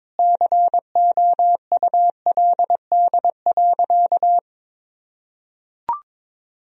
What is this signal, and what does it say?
Morse code, heard as a keyed tone, about 21 wpm, 700 Hz. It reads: COULD.
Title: Sentences from Top 1000 Words - Review Part 1 - 22wpm